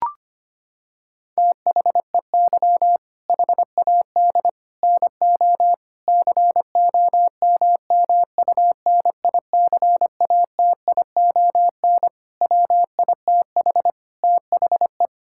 Morse code, heard as a keyed tone, about 25 wpm, 700 Hz. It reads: THEY HAD NO COMMUNICATION WITH THE
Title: Sentences from Top 1000 Words - Review Part 6 - 25wpm